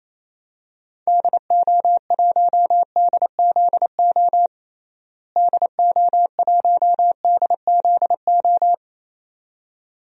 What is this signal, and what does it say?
Morse code, heard as a keyed tone, about 28 wpm, 700 Hz. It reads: DO1DZO DO1DZO
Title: DXCC-100 Call Signs - 6 Characters - Repeated 2x - 28wpm